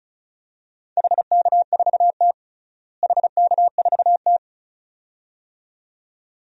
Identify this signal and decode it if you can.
Morse code, heard as a keyed tone, about 36 wpm, 700 Hz. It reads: HK4T HK4T
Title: DXCC-100 Call Signs - 4 Characters - Repeated 2x - 35wpm